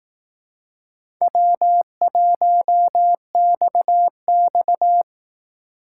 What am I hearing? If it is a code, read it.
Morse code, heard as a keyed tone, about 18 wpm, 700 Hz. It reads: W1XX
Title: Speed Racing - USA Call Signs 15wpm